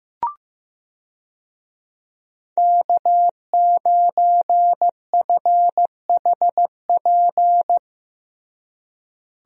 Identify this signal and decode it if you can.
Morse code, heard as a keyed tone, about 15 wpm, 700 Hz. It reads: K9FHP
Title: USA Call Signs - 5 Characters - Repeated 3x - 15wpm